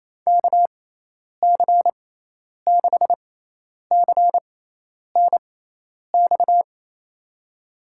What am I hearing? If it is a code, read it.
Morse code, heard as a keyed tone, about 28 wpm, spaced as for 10 wpm, 700 Hz. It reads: KC6CNX
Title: USA Call Signs - 6 Characters - 6x Character Spacing - 28wpm